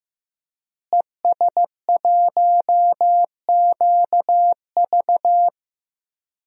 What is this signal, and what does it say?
Morse code, heard as a keyed tone, about 15 wpm, 700 Hz. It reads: ES1QV
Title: DXCC-100 Call Signs - 5 Characters - Repeated 3x - 15wpm